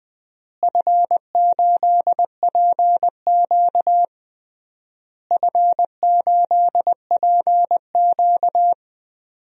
Morse code, heard as a keyed tone, about 19 wpm, 700 Hz. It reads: F8PQ F8PQ
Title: DXCC-100 Call Signs - 4 Characters - Repeated 3x - 20wpm